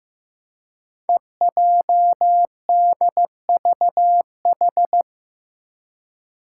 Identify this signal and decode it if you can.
Morse code, heard as a keyed tone, about 15 wpm, 700 Hz. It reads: EJDVH
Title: Five Challenging Letter-Number Combinations 15wpm